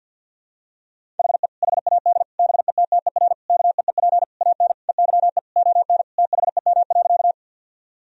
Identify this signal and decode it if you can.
Morse code, heard as a keyed tone, about 46 wpm, 700 Hz. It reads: HE HAD BETTER KEEP AN EYE ON THEM.
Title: Sentences from Top 1000 Words - Review Part 1 - 50wpm